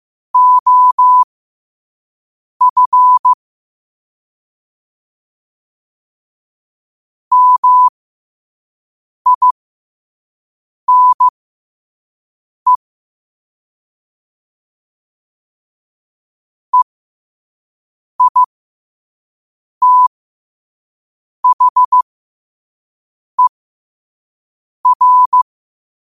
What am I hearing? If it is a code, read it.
Morse code, heard as a keyed tone, about 15 wpm, spaced as for 5 wpm, 1000 Hz. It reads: OF MINE EITHER
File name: qotdfiles / 5wpm